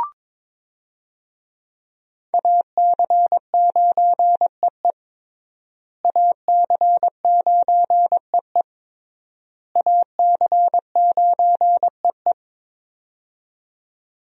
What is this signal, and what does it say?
Morse code, heard as a keyed tone, about 22 wpm, 700 Hz. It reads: AC9EE AC9EE AC9EE
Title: USA Call Signs - 5 Characters - Repeated 3x - 22wpm